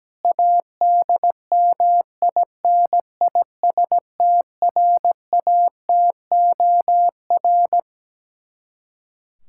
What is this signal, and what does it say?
Morse code, heard as a keyed tone, about 17 wpm, 700 Hz. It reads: ADMINISTRATOR
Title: K1USN SST QSO Practice 17wpm